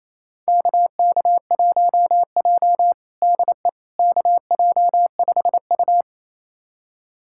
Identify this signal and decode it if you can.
Morse code, heard as a keyed tone, about 28 wpm, 700 Hz. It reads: KK1J DE KJ5U
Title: USA Callsign DE Callsign - 4 Characters - 28wpm